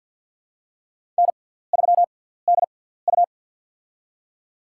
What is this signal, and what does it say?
Morse code, heard as a keyed tone, about 50 wpm, 700 Hz. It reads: N 3 D U
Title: USA Call Signs - 6x Character Spacing - 50wpm